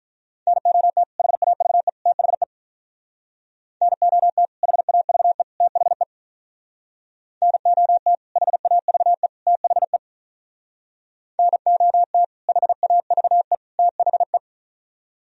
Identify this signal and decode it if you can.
Morse code, heard as a keyed tone, about 47 wpm, 700 Hz. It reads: NOT HAVE THE NOT HAVE THE NOT HAVE THE NOT HAVE THE